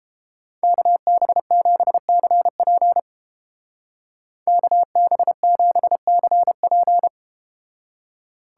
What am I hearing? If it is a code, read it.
Morse code, heard as a keyed tone, about 31 wpm, 700 Hz. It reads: KB7CP KB7CP